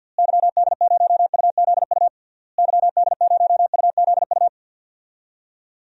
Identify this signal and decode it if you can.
Morse code, heard as a keyed tone, about 51 wpm, 700 Hz. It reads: YD0UZU YD0UZU
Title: DXCC-100 Call Signs - 6 Characters - Repeated 3x - 50wpm